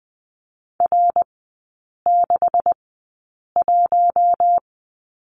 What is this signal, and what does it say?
Morse code, heard as a keyed tone, about 20 wpm, spaced as for 8 wpm, 690 Hz. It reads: R61
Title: 172 Lesson 28e - Number 9 - Three Characters 20wpm